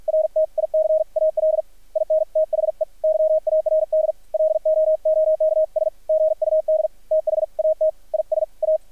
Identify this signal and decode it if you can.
Morse code, heard as a keyed tone, about 50 wpm, 630 Hz, with radio noise. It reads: RTICAL IN THE YARD LOOKS GUD THAT IS A
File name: archived / CRSnet